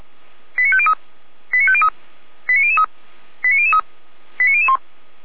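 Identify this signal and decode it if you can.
Five-tone selective call (ZVEI), heard as quick runs of five tones, tones of about 70 ms. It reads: 89503 89503 89003 89003 89001
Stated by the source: ZVEI five-tone selective call received with an SDR.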